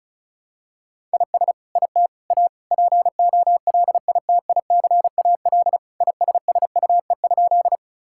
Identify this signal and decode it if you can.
Morse code, heard as a keyed tone, about 34 wpm, 700 Hz. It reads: IS IT A POLITICAL ISSUE?
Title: Sentences from Top 1000 Words - Review Part 6 - 35wpm